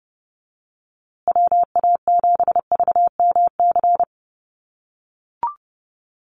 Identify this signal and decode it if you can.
Morse code, heard as a keyed tone, about 30 wpm, 700 Hz. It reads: WA7VMC
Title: USA Call Signs Repeated 3x 30wpm